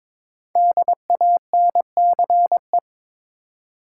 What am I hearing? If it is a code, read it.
Morse code, heard as a keyed tone, about 22 wpm, 700 Hz. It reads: DANCE
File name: 5 Letter Words - First 3 Letters Given - 22wpm